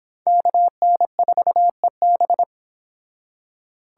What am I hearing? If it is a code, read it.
Morse code, heard as a keyed tone, about 30 wpm, 700 Hz. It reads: KN4EB